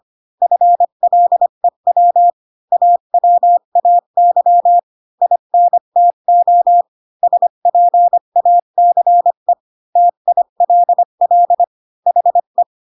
Morse code, heard as a keyed tone, about 25 wpm, 700 Hz. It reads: FLEW AWAY INTO SPACE TILL HE
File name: Pinocchio / Pinocchio ch5 20 25 ik1hge